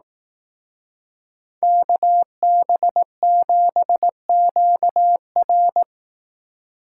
Morse code, heard as a keyed tone, about 18 wpm, 700 Hz. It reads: KB7QR
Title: Speed Racing - USA 2x2 Call Signs 15wpm